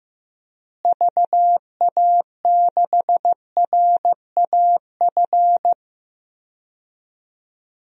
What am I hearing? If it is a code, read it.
Morse code, heard as a keyed tone, about 15 wpm, 700 Hz. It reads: VA6RAF
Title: DXCC-100 Call Signs - 6 Characters - Repeated 2x - 15wpm